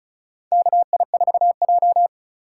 Morse code, heard as a keyed tone, about 34 wpm, 700 Hz. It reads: KI4J
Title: USA Call Signs - 4 Characters - 4x Character Spacing - 35wpm